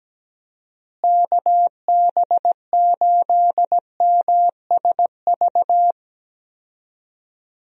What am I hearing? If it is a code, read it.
Morse code, heard as a keyed tone, about 17 wpm, 700 Hz. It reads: KB8MSV
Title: USA Call Signs - 6 Characters - Repeated 2x - 17wpm